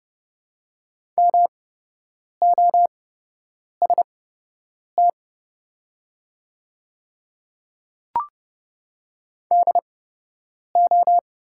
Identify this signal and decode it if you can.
Morse code, heard as a keyed tone, about 30 wpm, spaced as for 8 wpm, 700 Hz. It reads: MOST DO
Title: 8x Character Spacing - Top 200 Words - 30wpm